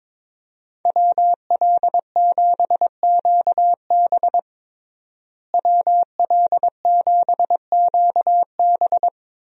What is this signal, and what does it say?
Morse code, heard as a keyed tone, about 22 wpm, 700 Hz. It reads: WL7QB WL7QB
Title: USA Call Signs - 5 Characters - Repeated 2x - 22wpm